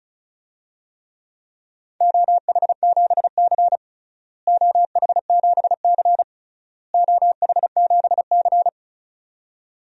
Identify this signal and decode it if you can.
Morse code, heard as a keyed tone, about 36 wpm, 700 Hz. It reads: OH7C OH7C OH7C
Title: DXCC-100 Call Signs - 4 Characters - Repeated 3x - 35wpm